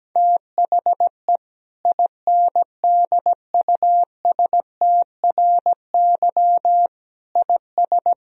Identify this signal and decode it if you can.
Morse code, heard as a keyed tone, about 17 wpm, 700 Hz. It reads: THE INDUSTRY IS
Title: Sentences from Top 1000 Words - Review Part 4 - 17wpm